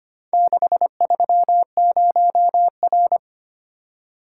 Morse code, heard as a keyed tone, about 25 wpm, 700 Hz. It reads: 630R
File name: practice / 25wpm / Four Letter-Number Combinations - 25wpm